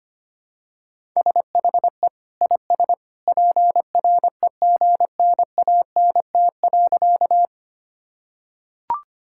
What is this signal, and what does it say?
Morse code, heard as a keyed tone, about 25 wpm, 700 Hz. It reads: SHE IS PREGNANT.
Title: Sentences from Top 1000 Words - Review Part 6 - 25wpm